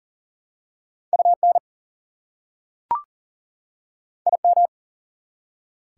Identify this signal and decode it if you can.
Morse code, heard as a keyed tone, about 40 wpm, 700 Hz. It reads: UN IM
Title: Prefixes and Suffixes 40wpm